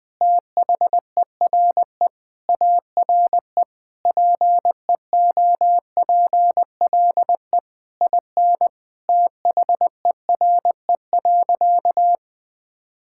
Morse code, heard as a keyed tone, about 20 wpm, 700 Hz. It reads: THERE ARE PEOPLE IN THERE.
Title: Sentences with 3-Word Phrases - 20wpm